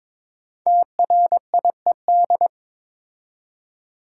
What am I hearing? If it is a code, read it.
Morse code, heard as a keyed tone, about 22 wpm, 700 Hz. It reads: TRIED
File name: Build A Phrase - 2-Word Phrases - 22wpm